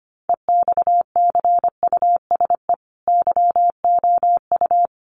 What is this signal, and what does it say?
Morse code, heard as a keyed tone, about 25 wpm, 700 Hz. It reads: EXCUSE YOU
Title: Sentences from Top 700 Words 25wpm